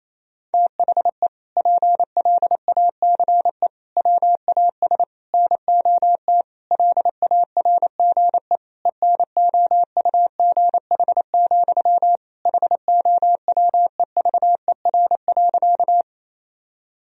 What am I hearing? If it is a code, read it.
Morse code, heard as a keyed tone, about 28 wpm, 700 Hz. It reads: THE PLACE WAS NOT LARGE ENOUGH, HOWEVER.
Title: Sentences from Top 1000 Words - Review Part 1 - 28wpm